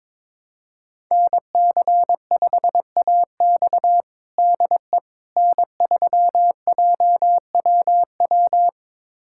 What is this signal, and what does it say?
Morse code, heard as a keyed tone, about 22 wpm, 700 Hz. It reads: NC5AX DE N3JWW
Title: USA Callsign DE Callsign - 5 Characters - 22wpm